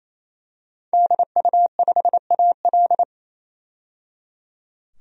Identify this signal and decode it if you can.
Morse code, heard as a keyed tone, about 28 wpm, 700 Hz. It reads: DU5AL
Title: DXCC-100 Call Signs - 5 Characters - 28wpm